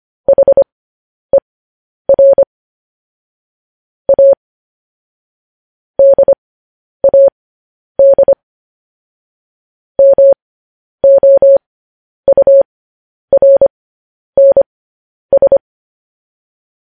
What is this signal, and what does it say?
Morse code, heard as a keyed tone, about 25 wpm, spaced as for 10 wpm, 550 Hz. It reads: HER A DAD MOURNS